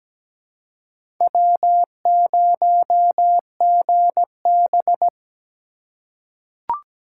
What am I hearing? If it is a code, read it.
Morse code, heard as a keyed tone, about 17 wpm, 700 Hz. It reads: W0GB
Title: USA Call Signs - 4 Characters - 3x Character Spacing - 17wpm